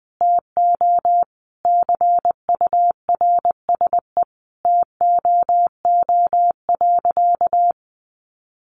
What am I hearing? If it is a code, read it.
Morse code, heard as a keyed tone, about 20 wpm, 700 Hz. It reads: TO CURSE TOO.
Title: A Sound Of Thunder 20wpm